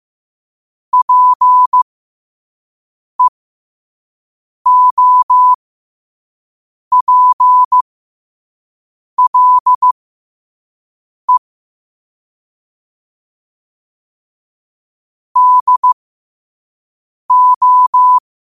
Morse code, heard as a keyed tone, about 15 wpm, spaced as for 5 wpm, 1000 Hz. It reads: PEOPLE DO
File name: qotdfiles / 5wpm